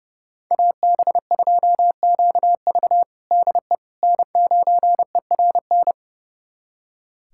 Morse code, heard as a keyed tone, about 30 wpm, 700 Hz. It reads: AB2QV DE N9ERN
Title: USA Callsign DE Callsign - 5 Characters - 30wpm